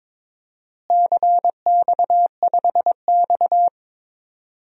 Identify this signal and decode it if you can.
Morse code, heard as a keyed tone, about 22 wpm, 700 Hz. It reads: CX5X
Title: DXCC-100 Call Signs - 4 Characters - Repeated 2x - 22wpm